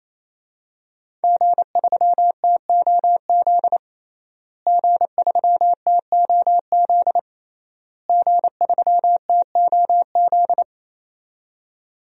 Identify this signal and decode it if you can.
Morse code, heard as a keyed tone, about 28 wpm, 700 Hz. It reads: G3TOZ G3TOZ G3TOZ
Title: DXCC-100 Call Signs - 5 Characters - Repeated 3x - 28wpm